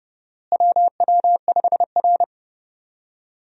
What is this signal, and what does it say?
Morse code, heard as a keyed tone, about 30 wpm, 700 Hz. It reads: WW5R
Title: USA Call Signs - 4 Characters - Repeated 2x - 30wpm